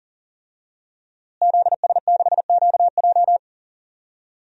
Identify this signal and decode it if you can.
Morse code, heard as a keyed tone, about 40 wpm, 700 Hz. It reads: ZS6QJ
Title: DXCC-100 Call Signs - 5 Characters - Repeated 2x - 40wpm